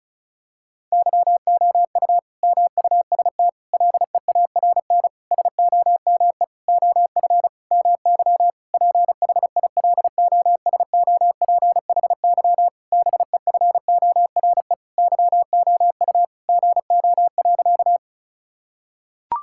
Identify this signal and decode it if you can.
Morse code, heard as a keyed tone, about 36 wpm, 700 Hz. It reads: YOU MUST LEARN SOME OF MY PHILOSOPHY BEFORE YOU GO.
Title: Sentences from Top 1000 Words - Review Part 6 - 35wpm